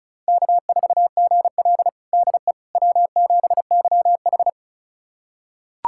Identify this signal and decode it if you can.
Morse code, heard as a keyed tone, about 36 wpm, 700 Hz. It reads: K4GL DE W7YH
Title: USA Callsign DE Callsign - 4 Characters - 35wpm